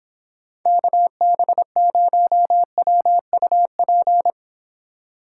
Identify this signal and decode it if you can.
Morse code, heard as a keyed tone, about 26 wpm, 700 Hz. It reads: KB0WUP